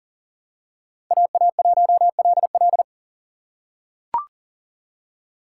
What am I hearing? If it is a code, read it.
Morse code, heard as a keyed tone, about 40 wpm, 700 Hz. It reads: AA1LL
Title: USA Call Signs - 5 Characters - 6x Character Spacing - 40wpm